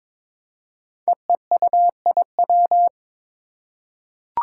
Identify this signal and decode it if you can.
Morse code, heard as a keyed tone, about 22 wpm, 700 Hz. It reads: EEUIW
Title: Five Challenging Letter-Number Combinations 22wpm